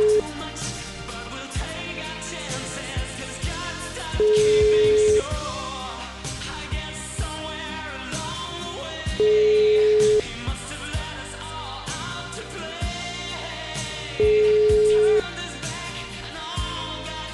Ringback tone (Europe, 425 Hz), with noise.